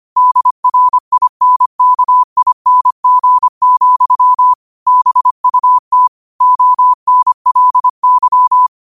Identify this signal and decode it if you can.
Morse code, heard as a keyed tone, about 25 wpm, 1000 Hz. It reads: DRINKING, BUT ONLY
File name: qotdfiles / 25wpm